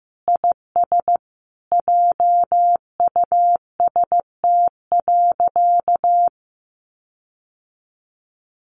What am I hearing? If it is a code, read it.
Morse code, heard as a keyed tone, about 14 wpm, 700 Hz. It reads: IS JUST.
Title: Sentences from Top 1500 Words 15wpm